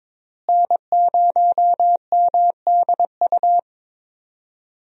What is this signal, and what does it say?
Morse code, heard as a keyed tone, about 22 wpm, 700 Hz. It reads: N0MDU
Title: USA Call Signs - 5 Characters - Repeated 3x - 22wpm